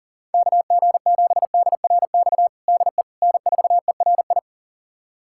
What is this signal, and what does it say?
Morse code, heard as a keyed tone, about 37 wpm, 700 Hz. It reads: KG7DRX DE N4ERI
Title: USA Callsign DE Callsign 40wpm